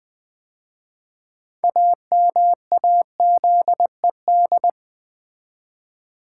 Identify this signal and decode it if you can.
Morse code, heard as a keyed tone, about 20 wpm, 700 Hz. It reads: AMAZED
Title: 6 Letter Words - First 3 Letters Given - 20wpm